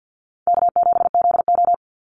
Morse code, heard as a keyed tone, about 49 wpm, 700 Hz. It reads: K7ZO
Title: NA SOTA Chaser Callsigns 50wpm